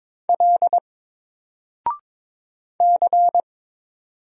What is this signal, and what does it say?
Morse code, heard as a keyed tone, about 22 wpm, 700 Hz. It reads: L C
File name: Single-Letter-Number-22wpm